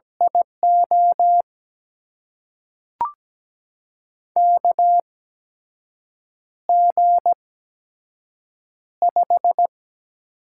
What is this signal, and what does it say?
Morse code, heard as a keyed tone, about 17 wpm, spaced as for 6 wpm, 700 Hz. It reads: IO KG5
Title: USA Call Signs - 6 Characters - 8x Character Spacing - 17wpm